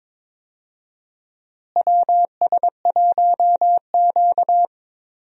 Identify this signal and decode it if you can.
Morse code, heard as a keyed tone, about 22 wpm, 700 Hz. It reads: WS1Q
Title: USA Call Signs - 4 Characters - Repeated 3x - 22wpm